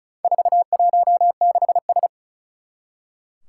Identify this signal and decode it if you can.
Morse code, heard as a keyed tone, about 36 wpm, 690 Hz. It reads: 416S
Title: Four Challenging Letter-Number Combinations 35wpm